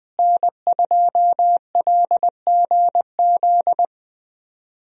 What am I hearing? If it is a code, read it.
Morse code, heard as a keyed tone, about 21 wpm, 700 Hz. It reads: N2LGZ
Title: USA Call Signs - 5 Characters - Repeated 3x - 20wpm